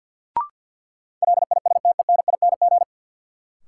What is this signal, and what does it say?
Morse code, heard as a keyed tone, about 51 wpm, 700 Hz. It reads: LISTENING
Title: Sets of 5 Words - New Word Drill - 50wpm